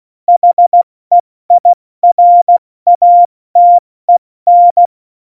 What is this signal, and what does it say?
Morse code, heard as a keyed tone, about 16 wpm, 700 Hz. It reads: HEIRATEN
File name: Morsekurs / Witze / Witz 16. Juli 2019, 70 F80